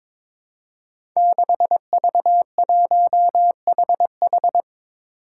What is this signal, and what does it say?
Morse code, heard as a keyed tone, about 22 wpm, 700 Hz. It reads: 6V1HH
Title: Five Challenging Letter-Number Combinations 22wpm